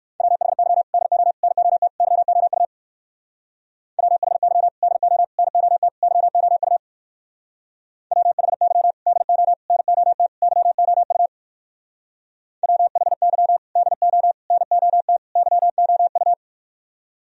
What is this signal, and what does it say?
Morse code, heard as a keyed tone, about 57 wpm, 700 Hz. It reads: WHY DO NOT YOU WHY DO NOT YOU WHY DO NOT YOU WHY DO NOT YOU